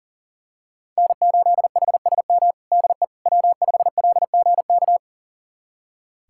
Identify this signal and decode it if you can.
Morse code, heard as a keyed tone, about 40 wpm, 700 Hz. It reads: N8HSM DE W5LGK
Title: USA Callsign DE Callsign - 5 Characters - 40wpm